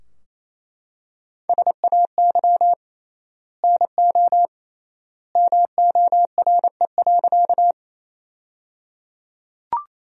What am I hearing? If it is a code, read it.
Morse code, heard as a keyed tone, about 28 wpm, 700 Hz. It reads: SAY NO MORE.
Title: 3x Word-Spacing - Sentences from Top 200 Words - 28wpm